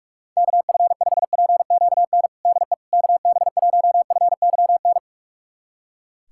Morse code, heard as a keyed tone, about 45 wpm, 700 Hz. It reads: KF5PQN DE KB1FYN
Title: USA Callsign DE Callsign - 6 Characters - 45wpm